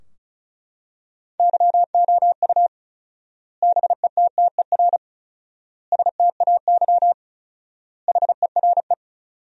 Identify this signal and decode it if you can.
Morse code, heard as a keyed tone, about 36 wpm, 700 Hz. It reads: YOU BETTER STAY HERE
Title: 4x Word-Spacing - Sentences from Top 500 Words - 35wpm